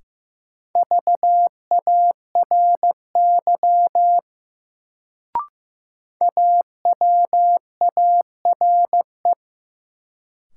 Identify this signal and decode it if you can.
Morse code, heard as a keyed tone, about 15 wpm, 700 Hz. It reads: VARY AWARE
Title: Sentences from Top 1200 Words - New Word Drill - 15wpm